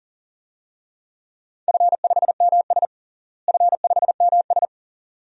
Morse code, heard as a keyed tone, about 37 wpm, 700 Hz. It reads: F5MS F5MS
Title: DXCC-100 Call Signs - 4 Characters - Repeated 3x - 40wpm